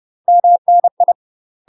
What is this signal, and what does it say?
Morse code, heard as a keyed tone, about 30 wpm, 700 Hz. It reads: MNI
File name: UK MASTER QSOs TRAINING / 30WPM / UK MASTER QSOs 0030